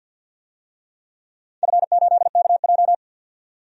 Morse code, heard as a keyed tone, about 51 wpm, 700 Hz. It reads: F8KJ
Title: DXCC-100 Call Signs - 4 Characters - 50wpm